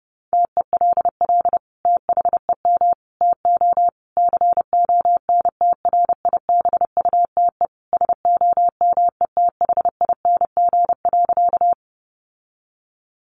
Morse code, heard as a keyed tone, about 30 wpm, 700 Hz. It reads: TELL THEM TO CONTRIBUTE SOMETHING.
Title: Sentences from Top 1400 Words 30wpm